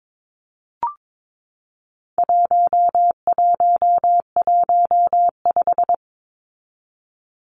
Morse code, heard as a keyed tone, about 22 wpm, 700 Hz. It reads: 1115
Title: Numbers - 4-Digit 22wpm